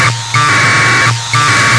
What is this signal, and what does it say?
Packet radio (AX.25, as in APRS).
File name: Sohla-1 23h48 utc 1200bd packet NormalModeby ja0caw